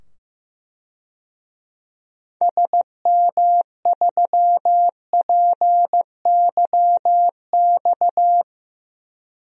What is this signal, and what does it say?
Morse code, heard as a keyed tone, about 15 wpm, 700 Hz. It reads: SM3PYX
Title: DXCC-100 Call Signs - 6 Characters - 15wpm